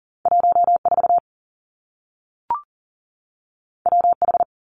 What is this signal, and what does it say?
Morse code, heard as a keyed tone, about 40 wpm, 700 Hz. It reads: 14 WH
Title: Two Challenging Letter-Number Combinations 40wpm